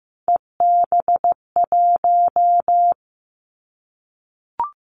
Morse code, heard as a keyed tone, about 15 wpm, 700 Hz. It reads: EB1
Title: Callsign Prefixes for Top 25 DXCC - 15wpm